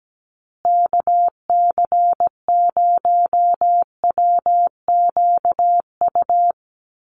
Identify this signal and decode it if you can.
Morse code, heard as a keyed tone, about 17 wpm, 700 Hz. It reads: KC0WQU
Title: USA Call Signs Repeated 2x 17wpm